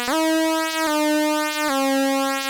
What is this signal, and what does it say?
There is no code.